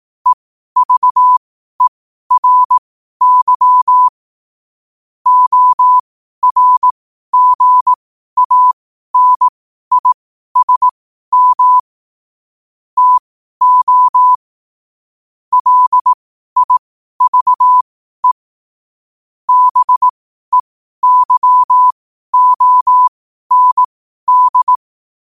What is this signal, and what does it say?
Morse code, heard as a keyed tone, about 18 wpm, spaced as for 12 wpm, 1000 Hz. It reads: EVERY ORGANISM TO LIVE BEYOND
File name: qotdfiles / 12wpm